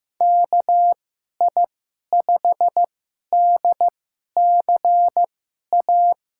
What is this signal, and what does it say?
Morse code, heard as a keyed tone, about 15 wpm, spaced as for 11 wpm, 700 Hz. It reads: KI5DCA
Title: USA Call Signs - 2x Character Spacing - 15wpm